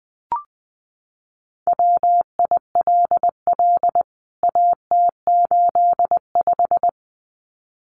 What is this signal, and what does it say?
Morse code, heard as a keyed tone, about 20 wpm, 700 Hz. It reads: WILL AT85
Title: CWT Exchange 20wpm